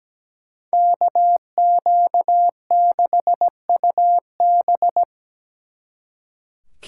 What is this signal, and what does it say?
Morse code, heard as a keyed tone, about 20 wpm, 700 Hz. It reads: KQ6UB
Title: Speed Racing - USA 2x2 Call Signs 17wpm